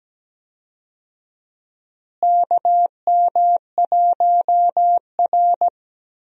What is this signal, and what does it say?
Morse code, heard as a keyed tone, about 17 wpm, 700 Hz. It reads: KM1R
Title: USA Call Signs - 4 Characters - Repeated 2x - 17wpm